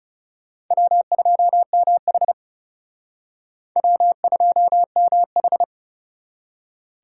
Morse code, heard as a keyed tone, about 40 wpm, 700 Hz. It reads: W2MH W2MH